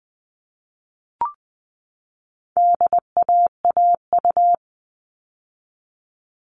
Morse code, heard as a keyed tone, about 20 wpm, 700 Hz. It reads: DAAU
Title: All Cut Numbers - 4-Digit 20wpm